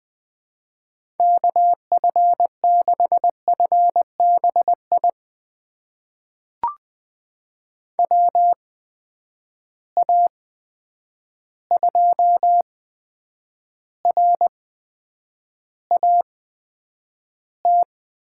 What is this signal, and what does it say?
Morse code, heard as a keyed tone, about 20 wpm, spaced as for 9 wpm, 700 Hz. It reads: KF6FBI WA2RAT
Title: USA Call Signs - 6 Characters - 8x Character Spacing - 20wpm